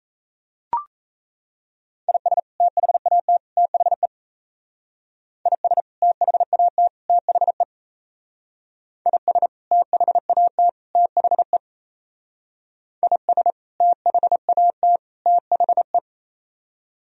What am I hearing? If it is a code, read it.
Morse code, heard as a keyed tone, about 34 wpm, 700 Hz. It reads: IS THAT THE IS THAT THE IS THAT THE IS THAT THE
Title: Speed-Racing - Sets of 3 Words 28wpm